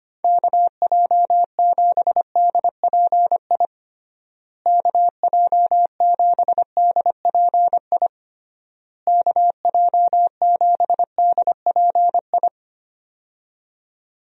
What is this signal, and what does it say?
Morse code, heard as a keyed tone, about 25 wpm, 700 Hz. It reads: KJ7DPI KJ7DPI KJ7DPI
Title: USA Call Signs - 6 Characters - Repeated 3x - 25wpm